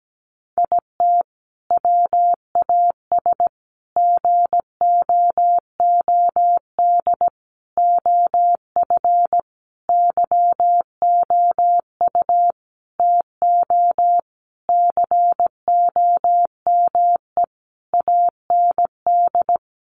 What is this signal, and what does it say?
Morse code, heard as a keyed tone, about 17 wpm, 700 Hz. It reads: IT WAS GOOD OF YOU TO COME AND
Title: Sentences from Top 1100 Words 17wpm